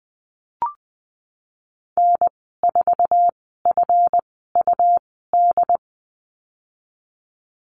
Morse code, heard as a keyed tone, about 20 wpm, spaced as for 15 wpm, 700 Hz. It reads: N4FUD
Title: 140 Lesson 23g - Letter f - Call Signs 20wpm